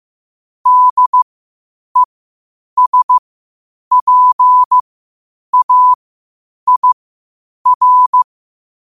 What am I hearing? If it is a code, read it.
Morse code, heard as a keyed tone, about 15 wpm, spaced as for 8 wpm, 1000 Hz. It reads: DESPAIR
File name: qotdfiles / 8wpm